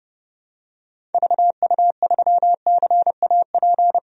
Morse code, heard as a keyed tone, about 30 wpm, 700 Hz. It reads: VU3CAP
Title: DXCC-100 Call Signs - 6 Characters - Repeated 3x - 30wpm